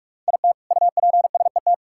Morse code, heard as a keyed tone, about 41 wpm, 700 Hz. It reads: IT UPSET
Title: Sentences from Top 1000 Words - Review Part 5 - 45wpm